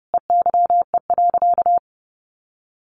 Morse code, heard as a keyed tone, about 29 wpm, 700 Hz. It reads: EYE.
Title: Sentences from Top 1400 Words 30wpm